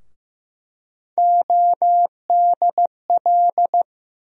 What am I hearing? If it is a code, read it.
Morse code, heard as a keyed tone, about 15 wpm, 700 Hz. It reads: ODL
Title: USA Call Sign Postfixes - 3 Letters 15wpm